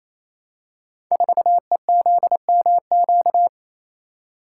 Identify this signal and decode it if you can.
Morse code, heard as a keyed tone, about 27 wpm, 690 Hz. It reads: 4EZMQ
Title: Code Groups - 5 Letter:Numbers - 28wpm